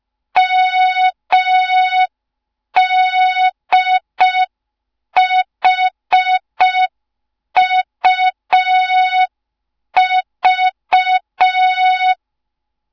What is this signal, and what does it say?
Morse code, heard as a keyed tone, about 5 wpm, 3700 Hz. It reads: MDHUV
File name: Morze-HA2MN / hang / 07 GYAKORLAT